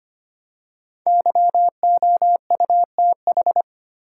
Morse code, heard as a keyed tone, about 25 wpm, 700 Hz. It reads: YOUTH
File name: 5 Letter Words - First 3 Letters Given - 25wpm